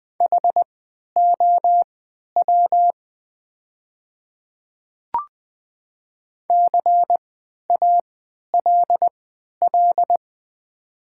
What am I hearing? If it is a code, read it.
Morse code, heard as a keyed tone, about 20 wpm, spaced as for 11 wpm, 700 Hz. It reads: HOW CALL
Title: 3x Character Spacing - Top 100 Words - 20wpm